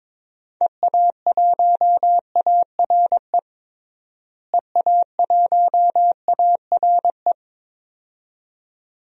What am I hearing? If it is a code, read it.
Morse code, heard as a keyed tone, about 22 wpm, 700 Hz. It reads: EA1ARE EA1ARE
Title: DXCC-100 Call Signs - 6 Characters - Repeated 2x - 22wpm